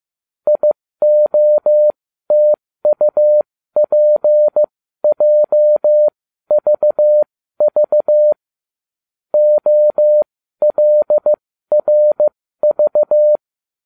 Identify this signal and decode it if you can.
Morse code, heard as a keyed tone, about 15 wpm, spaced as for 12 wpm, 600 Hz. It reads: IOTUPJVV OLRV